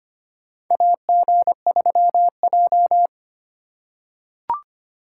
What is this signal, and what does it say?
Morse code, heard as a keyed tone, about 25 wpm, 700 Hz. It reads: AG3J
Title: USA Call Signs - 4 Characters - 8x Character Spacing - 25wpm